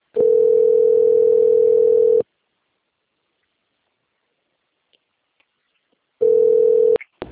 Ringback tone (North America), 440 + 480 Hz.